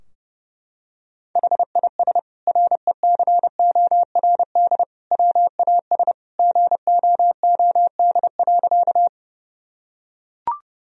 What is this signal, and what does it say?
Morse code, heard as a keyed tone, about 30 wpm, 700 Hz. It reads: HIS RECORD WAS GOOD.
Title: Sentences from Top 1000 Words - Review Part 3 - 30wpm